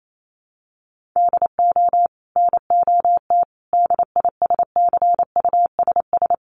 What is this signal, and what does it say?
Morse code, heard as a keyed tone, about 28 wpm, 700 Hz. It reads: DO NOT DISCUSS
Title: Sentences from Top 1200 Words 28wpm